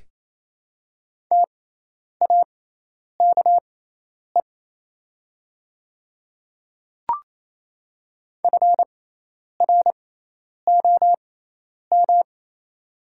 Morse code, heard as a keyed tone, about 28 wpm, spaced as for 10 wpm, 690 Hz. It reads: TAKE FROM
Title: 6x Character Spacing - Top 300 Words - 28wpm